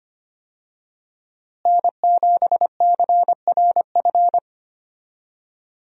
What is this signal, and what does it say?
Morse code, heard as a keyed tone, about 25 wpm, 700 Hz. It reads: N7CRF
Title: USA Call Signs - 5 Characters - Repeated 3x - 25wpm